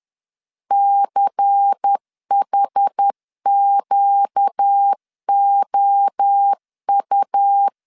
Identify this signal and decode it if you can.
Morse code, heard as a keyed tone, about 10 wpm, 800 Hz. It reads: CHQOU